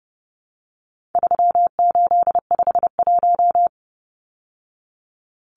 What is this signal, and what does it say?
Morse code, heard as a keyed tone, about 30 wpm, 700 Hz. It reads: 3851
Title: Numbers - 4-Digit 30wpm